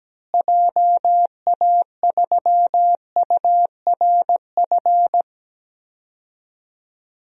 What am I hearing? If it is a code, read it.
Morse code, heard as a keyed tone, about 17 wpm, 700 Hz. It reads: JA3URF
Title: DXCC-100 Call Signs - 6 Characters - Repeated 2x - 17wpm